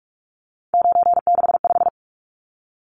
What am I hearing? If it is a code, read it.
Morse code, heard as a keyed tone, about 44 wpm, 700 Hz. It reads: 965
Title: Numbers - 3-Digit 45wpm